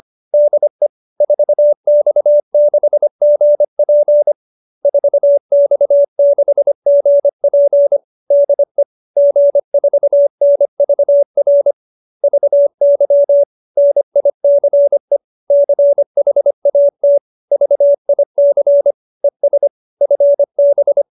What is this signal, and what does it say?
Morse code, heard as a keyed tone, about 25 wpm, 570 Hz. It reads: DE 4X6GP 4X6GP DE G4NVR VY NICE CHAT VIC ES FB
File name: QSO315 25